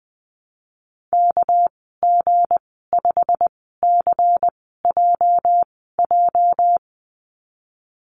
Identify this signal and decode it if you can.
Morse code, heard as a keyed tone, about 20 wpm, spaced as for 15 wpm, 700 Hz. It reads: KG5CJJ
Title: 219 Lesson 35g - Number 0 - Call Signs 20wpm